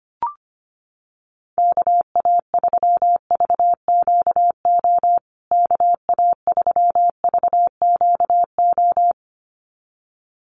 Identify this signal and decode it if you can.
Morse code, heard as a keyed tone, about 25 wpm, 700 Hz. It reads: KA3VQO KA3VQO
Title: USA Call Signs Repeated 2x 25wpm